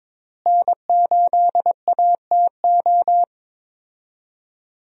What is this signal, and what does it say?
Morse code, heard as a keyed tone, about 22 wpm, 700 Hz. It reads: N8ATO
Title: USA Call Signs - 5 Characters - Repeated 3x - 22wpm